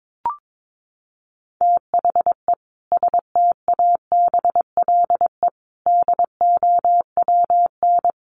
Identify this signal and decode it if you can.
Morse code, heard as a keyed tone, about 22 wpm, 700 Hz. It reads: THE STABLE DOWN
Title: Sentences from Top 1500 Words 22wpm